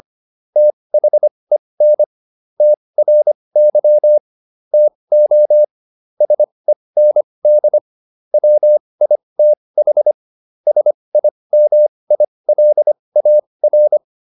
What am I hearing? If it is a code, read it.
Morse code, heard as a keyed tone, about 25 wpm, spaced as for 20 wpm, 600 Hz. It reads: THEN TRY TO SEND WITH SIMILAR